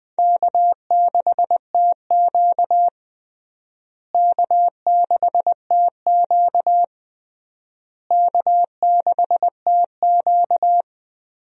Morse code, heard as a keyed tone, about 21 wpm, 700 Hz. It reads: K6TQ K6TQ K6TQ
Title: USA Call Signs - 4 Characters - Repeated 3x - 20wpm